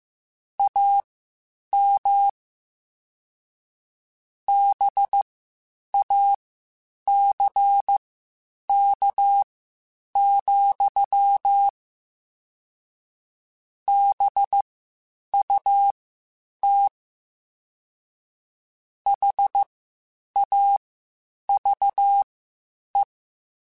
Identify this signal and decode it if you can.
Morse code, heard as a keyed tone, about 15 wpm, spaced as for 8 wpm, 800 Hz. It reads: AM BACK, BUT HAVE